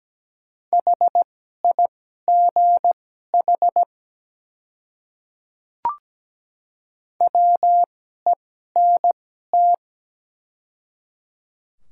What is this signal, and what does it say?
Morse code, heard as a keyed tone, about 17 wpm, spaced as for 12 wpm, 700 Hz. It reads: HIGH WENT
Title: 2x Character Spacing - Top 200 Words - 17wpm